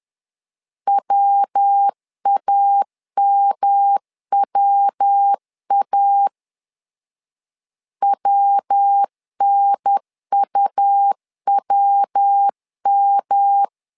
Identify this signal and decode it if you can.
Morse code, heard as a keyed tone, about 10 wpm, 800 Hz. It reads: WAMWA WNUWM
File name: LEZIONI / 3 MWRO / 3 MW 10wpm 1959